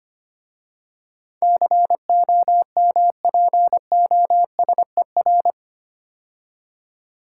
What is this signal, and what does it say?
Morse code, heard as a keyed tone, about 25 wpm, 700 Hz. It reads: COMPOSER
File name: 8 Letter Words - First 3 Letters Given - 25wpm